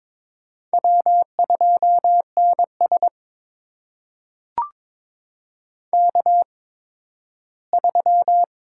Morse code, heard as a keyed tone, about 22 wpm, spaced as for 13 wpm, 700 Hz. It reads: W2NS K3
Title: USA Call Signs - 8x Character Spacing - 22wpm